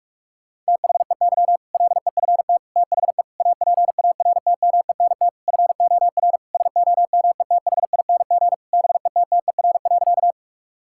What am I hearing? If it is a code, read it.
Morse code, heard as a keyed tone, about 46 wpm, 700 Hz. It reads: THEY LEFT THE APARTMENT FOR SOMETHING BETTER.
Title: Sentences from Top 1000 Words - Review Part 6 - 45wpm